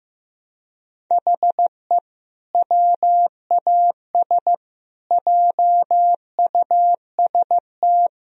Morse code, heard as a keyed tone, about 16 wpm, 700 Hz. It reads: HE WAS JUST
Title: Sentences from Top 1000 Words - Review Part 2 - 15wpm